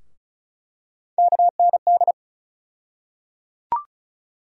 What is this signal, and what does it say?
Morse code, heard as a keyed tone, about 36 wpm, 700 Hz. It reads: KND
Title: USA Call Sign Postfixes - 3 Letters 35wpm